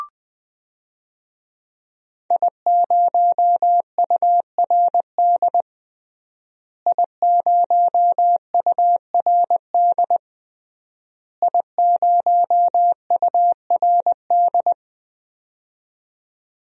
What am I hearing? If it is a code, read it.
Morse code, heard as a keyed tone, about 20 wpm, 700 Hz. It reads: I0URD I0URD I0URD
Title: DXCC-100 Call Signs - 5 Characters - Repeated 3x - 20wpm